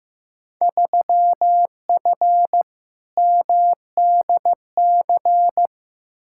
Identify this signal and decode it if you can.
Morse code, heard as a keyed tone, about 15 wpm, 700 Hz. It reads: 3F MDC
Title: Field Day Exchanges Encore - 15wpm